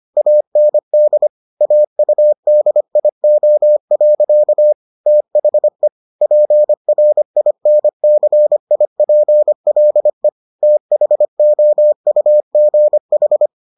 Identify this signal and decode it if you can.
Morse code, heard as a keyed tone, about 25 wpm, 600 Hz. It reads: AND AUDIO. THE PRINCIPLE THOUGH